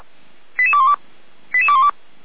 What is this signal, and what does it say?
Five-tone selective call (ZVEI), heard as quick runs of five tones, tones of about 70 ms. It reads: 88212 88212